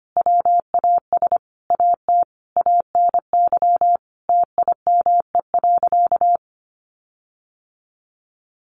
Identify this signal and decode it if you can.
Morse code, heard as a keyed tone, about 25 wpm, 700 Hz. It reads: WAS AT ANY TIME.
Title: Sentences from Top 1500 Words 25wpm